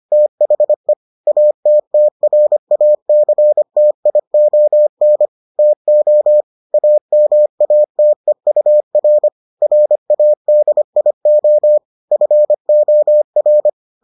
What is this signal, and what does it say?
Morse code, heard as a keyed tone, about 25 wpm, 600 Hz. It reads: THE ATTRACTION TO AMATEUR RADIO FOR